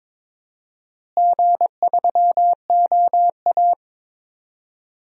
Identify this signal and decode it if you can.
Morse code, heard as a keyed tone, about 21 wpm, 700 Hz. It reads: G3OA
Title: DXCC-100 Call Signs - 4 Characters - Repeated 3x - 22wpm